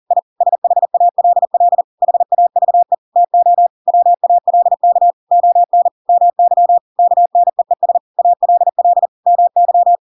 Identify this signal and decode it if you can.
Morse code, heard as a keyed tone, about 40 wpm, 700 Hz. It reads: I SHALL HAVE TO WALK ON MY KNEES ALL MY
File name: Pinocchio / Pinocchio ch7 40 40 ik1hge